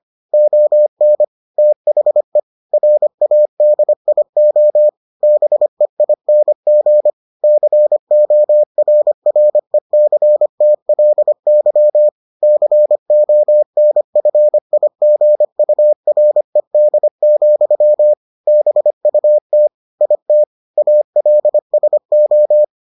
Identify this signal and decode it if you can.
Morse code, heard as a keyed tone, about 25 wpm, 600 Hz. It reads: ON THE RADIO BEING CORRECTLY CONFIGURED, BUT IT ALSO